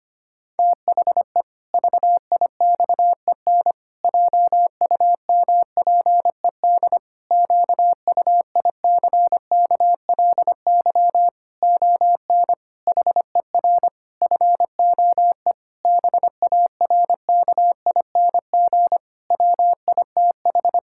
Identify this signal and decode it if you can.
Morse code, heard as a keyed tone, about 25 wpm, 700 Hz. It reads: THE VIXEN JUMPED QUICKLY ON HER FOE BARKING WITH
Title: Pangram Challenge v2 - 25wpm